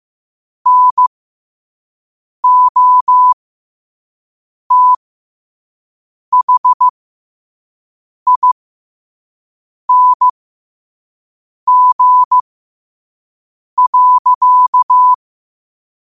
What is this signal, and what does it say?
Morse code, heard as a keyed tone, about 15 wpm, spaced as for 5 wpm, 1000 Hz. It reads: NOTHING.
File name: qotdfiles / 5wpm